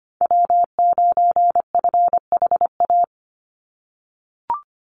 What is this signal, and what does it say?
Morse code, heard as a keyed tone, about 25 wpm, 700 Hz. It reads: W9FHA
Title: NA SOTA Chaser Callsigns 25wpm